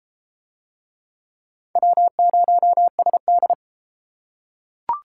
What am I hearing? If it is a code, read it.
Morse code, heard as a keyed tone, about 33 wpm, 700 Hz. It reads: W0SD